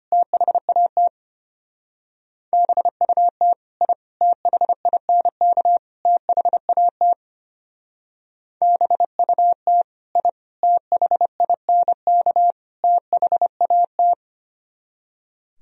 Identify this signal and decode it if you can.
Morse code, heard as a keyed tone, about 31 wpm, 700 Hz. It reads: THAT BUT I THINK THAT BUT I THINK THAT
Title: Speed-Racing - Sets of 4 Words 25wpm